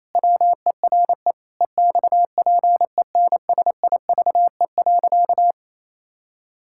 Morse code, heard as a keyed tone, about 28 wpm, 700 Hz. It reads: WERE EXPENSIVE.
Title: Pangram Challenge v2 - 28wpm